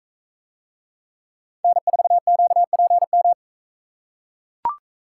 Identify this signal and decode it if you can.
Morse code, heard as a keyed tone, about 43 wpm, 700 Hz. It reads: N4QPM